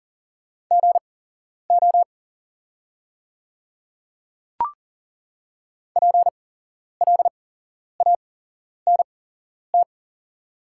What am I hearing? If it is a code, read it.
Morse code, heard as a keyed tone, about 40 wpm, spaced as for 11 wpm, 700 Hz. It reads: GO PLANT
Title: 8x Character Spacing - Top 300 Words - 40wpm